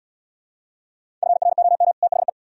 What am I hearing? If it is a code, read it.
Morse code, heard as a keyed tone, about 76 wpm, 700 Hz. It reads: FROM THE